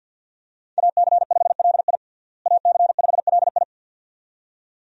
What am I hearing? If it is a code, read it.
Morse code, heard as a keyed tone, about 47 wpm, 700 Hz. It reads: AC5LI AC5LI
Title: USA Call Signs - 5 Characters - Repeated 2x - 50wpm